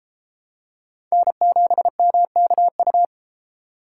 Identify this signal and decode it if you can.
Morse code, heard as a keyed tone, about 33 wpm, 700 Hz. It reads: N7MKU